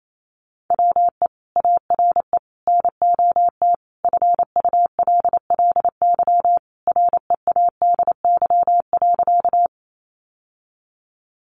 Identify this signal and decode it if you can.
Morse code, heard as a keyed tone, about 28 wpm, 700 Hz. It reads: WE ARE NOT FULLY READY.
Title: Sentences from Top 1300 Words 28wpm